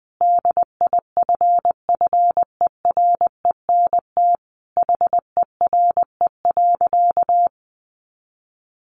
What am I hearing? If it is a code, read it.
Morse code, heard as a keyed tone, about 20 wpm, 700 Hz. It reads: DIFFERENT HERE.
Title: Easy Phrases 20wpm